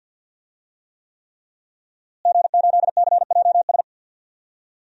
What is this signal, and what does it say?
Morse code, heard as a keyed tone, about 52 wpm, 700 Hz. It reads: G8CJS
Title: DXCC-100 Call Signs - 5 Characters - Repeated 3x - 50wpm